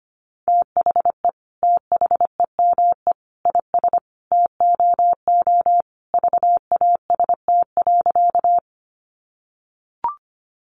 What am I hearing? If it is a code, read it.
Morse code, heard as a keyed tone, about 25 wpm, 700 Hz. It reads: THE THEME IS TOO VAST.
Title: Sentences from Top 1300 Words 25wpm